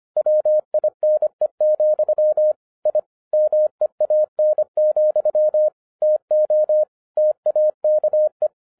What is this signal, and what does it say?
Morse code, heard as a keyed tone, about 26 wpm, 610 Hz. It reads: WINE, I MEAN, TO TAKE